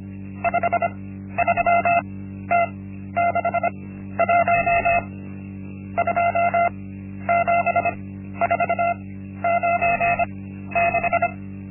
Morse code, heard as a keyed tone, about 25 wpm, spaced as for 12 wpm, 690 Hz. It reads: 53T61 27496
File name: Linksmakalnis-1